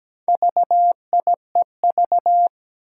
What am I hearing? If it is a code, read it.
Morse code, heard as a keyed tone, about 17 wpm, 700 Hz. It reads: VIEV
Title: Four Challenging Letter-Number Combinations 17wpm